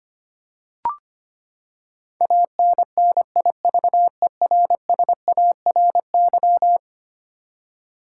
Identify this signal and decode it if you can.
Morse code, heard as a keyed tone, about 25 wpm, 700 Hz. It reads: ANNIVERSARY
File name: practice / 25wpm / Eleven Letter Words - 25wpm